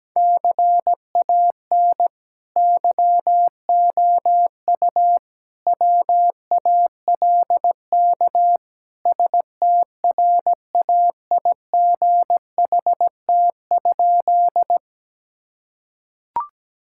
Morse code, heard as a keyed tone, about 17 wpm, 700 Hz. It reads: CAN YOU WALK STRAIGHT?
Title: Sentences from Top 1000 Words - Review Part 6 - 17wpm